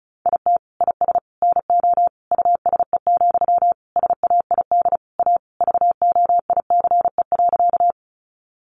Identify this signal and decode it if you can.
Morse code, heard as a keyed tone, about 34 wpm, 700 Hz. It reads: IT IS NO USE, SAID A VOICE.
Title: Sentences from Top 500 Words 35wpm